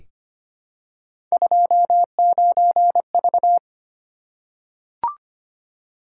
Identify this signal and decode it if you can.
Morse code, heard as a keyed tone, about 25 wpm, 700 Hz. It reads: 29V
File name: Three Letter-Number Combinations - 25wpm